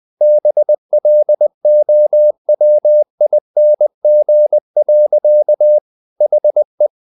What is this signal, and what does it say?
Morse code, heard as a keyed tone, about 20 wpm, 600 Hz. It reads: BLOWING. HE
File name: the old man and the sea / the old man and the sea part 1 20wpm